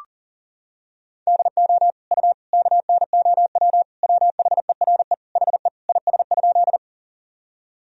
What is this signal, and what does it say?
Morse code, heard as a keyed tone, about 40 wpm, 700 Hz. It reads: DO U KNOW WHERE HE IS?
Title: Sentences from Top 300 Words and Abbreviations 40wpm